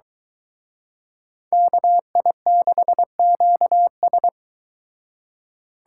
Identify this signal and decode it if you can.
Morse code, heard as a keyed tone, about 23 wpm, 700 Hz. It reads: KI6QS